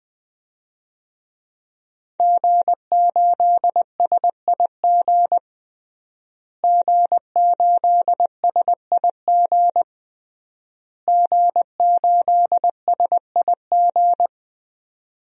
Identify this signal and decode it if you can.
Morse code, heard as a keyed tone, about 20 wpm, 700 Hz. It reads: G8SIG G8SIG G8SIG
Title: DXCC-100 Call Signs - 5 Characters - Repeated 3x - 20wpm